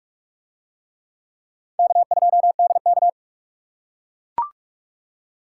Morse code, heard as a keyed tone, about 44 wpm, 700 Hz. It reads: K2DK